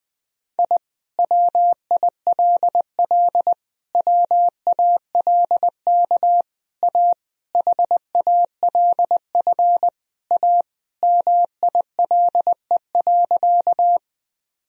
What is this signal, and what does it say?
Morse code, heard as a keyed tone, about 20 wpm, 700 Hz. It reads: I WILL WALK A HALF A MILE.
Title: Sentences from Top 1000 Words - Review Part 1 - 20wpm